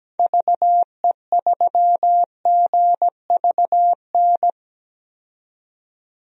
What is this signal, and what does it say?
Morse code, heard as a keyed tone, about 17 wpm, 700 Hz. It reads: VE3GVN
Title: DXCC-100 Call Signs - 6 Characters - 17wpm